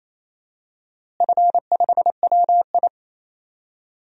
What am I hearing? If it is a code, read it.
Morse code, heard as a keyed tone, about 27 wpm, 700 Hz. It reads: F5WI
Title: DXCC-100 Call Signs - 4 Characters - Repeated 3x - 28wpm